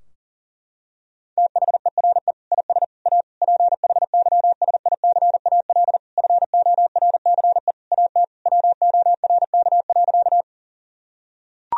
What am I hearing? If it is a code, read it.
Morse code, heard as a keyed tone, about 40 wpm, 700 Hz. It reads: THERE IS A PHYSICAL FORCE AT WORK.
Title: Sentences from Top 1000 Words - Review Part 4 - 40wpm